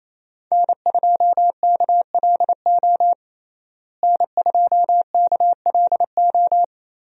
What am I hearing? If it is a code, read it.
Morse code, heard as a keyed tone, about 28 wpm, 700 Hz. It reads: N2KLO N2KLO
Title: USA Call Signs - 5 Characters - Repeated 3x - 28wpm